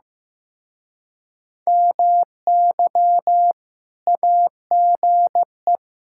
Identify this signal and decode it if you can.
Morse code, heard as a keyed tone, about 15 wpm, 700 Hz. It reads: MY AGE
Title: Speed-Racing Common QSO Phrases Encore - 15wpm